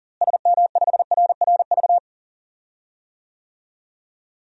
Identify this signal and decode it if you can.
Morse code, heard as a keyed tone, about 40 wpm, 700 Hz. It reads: SM5RRV
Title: DXCC-100 Call Signs - 6 Characters - Repeated 3x - 40wpm